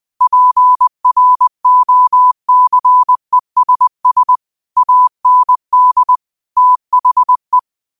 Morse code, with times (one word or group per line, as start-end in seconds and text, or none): 0.2-4.4 PROCESS
4.8-6.2 AND
6.6-7.6 THE